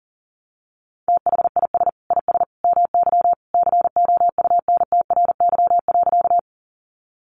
Morse code, heard as a keyed tone, about 40 wpm, 700 Hz. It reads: THIS IS MY COUNTRY.
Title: Sentences from Top 300 Words 40wpm